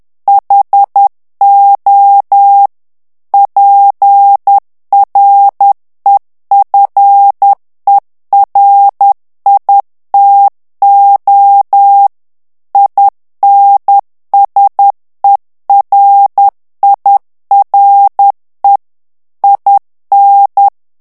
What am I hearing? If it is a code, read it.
Morse code, heard as a keyed tone, about 10 wpm, 800 Hz. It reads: HO PREFERITO INSERIRE IN
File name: morse mp3 / ITALIEN texte clair 1 vitesse 10 mots minute